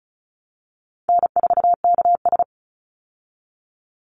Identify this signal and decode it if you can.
Morse code, heard as a keyed tone, about 34 wpm, 700 Hz. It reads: N4KS
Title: Easy Call Signs Repeated 2x - 35wpm